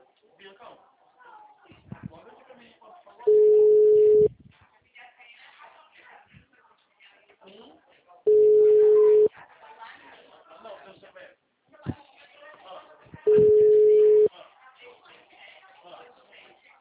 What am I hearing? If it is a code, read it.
Ringback tone (Europe, 425 Hz), with speech.